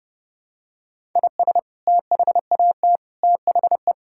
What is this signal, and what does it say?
Morse code, heard as a keyed tone, about 30 wpm, 700 Hz. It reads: IS THAT THE
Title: Sentences from Top 1000 Words - Review Part 5 - 30wpm